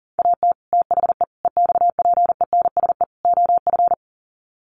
Morse code, heard as a keyed tone, about 40 wpm, 700 Hz. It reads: AT THE EXPENSE OF
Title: Sets of 4 Words 40wpm